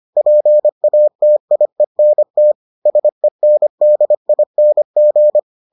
Morse code, heard as a keyed tone, about 25 wpm, 600 Hz. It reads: PATIENT SENDING